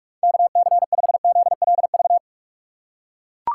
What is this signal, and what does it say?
Morse code, heard as a keyed tone, about 42 wpm, 700 Hz. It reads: KC5ZLV
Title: USA Call Signs - 3x Character Spacing - 45wpm